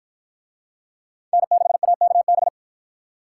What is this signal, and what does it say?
Morse code, heard as a keyed tone, about 53 wpm, 700 Hz. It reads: N6AKB